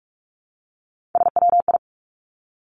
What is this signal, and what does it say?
Morse code, heard as a keyed tone, about 46 wpm, 700 Hz. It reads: SWI
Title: Three Challenging Letter-Number Combinations 45wpm